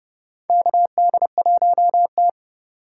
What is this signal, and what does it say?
Morse code, heard as a keyed tone, about 30 wpm, 700 Hz. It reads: KD1T
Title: USA Call Signs - 4 Characters - 2x Character Spacing - 30wpm